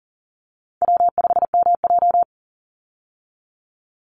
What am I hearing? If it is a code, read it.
Morse code, heard as a keyed tone, about 41 wpm, 700 Hz. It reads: W5MJ
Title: Easy Call Signs - 40wpm